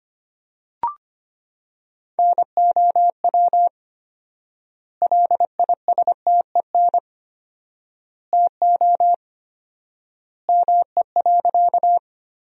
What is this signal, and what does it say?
Morse code, heard as a keyed tone, about 25 wpm, 700 Hz. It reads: NOW LISTEN TO ME.
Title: 4x Word-Spacing - Sentences from Top 400 Words - 25wpm